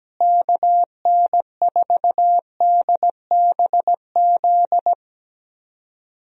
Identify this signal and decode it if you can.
Morse code, heard as a keyed tone, about 17 wpm, 700 Hz. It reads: KN4DBZ
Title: USA Call Signs - 6 Characters - Repeated 2x - 17wpm